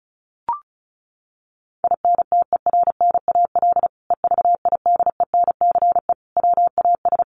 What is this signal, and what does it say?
Morse code, heard as a keyed tone, about 34 wpm, 700 Hz. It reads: INTERNAL EVIDENCE WAS
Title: Sentences from Top 1400 Words 35wpm